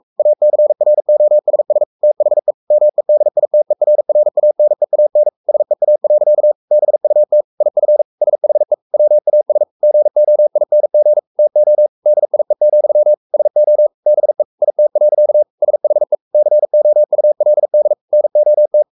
Morse code, heard as a keyed tone, about 44 wpm, 600 Hz. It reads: ACROSS THE MEDITERRANEAN SEA. BUT IF SHE WAS GOING TO DIE, SO BE IT. SHE COULD NOT
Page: Romanen Never af Ken Follet på engelsk.